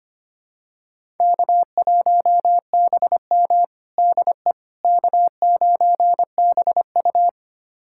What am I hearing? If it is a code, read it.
Morse code, heard as a keyed tone, about 25 wpm, 700 Hz. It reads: K1BM DE K9BU
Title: USA Callsign DE Callsign - 4 Characters - 25wpm